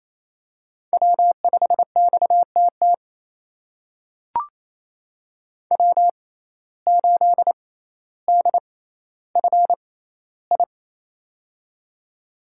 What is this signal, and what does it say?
Morse code, heard as a keyed tone, about 28 wpm, spaced as for 14 wpm, 700 Hz. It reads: W5XTT W8DFI
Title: USA Call Signs - 5 Characters - 6x Character Spacing - 28wpm